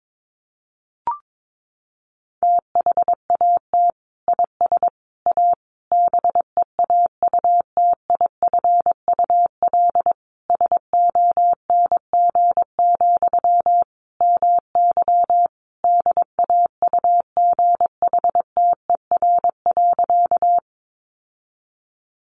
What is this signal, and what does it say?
Morse code, heard as a keyed tone, about 22 wpm, 700 Hz. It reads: THAT IS A BEAUTIFUL SONG, MY DAUGHTER.
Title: Sentences from Top 1500 Words 22wpm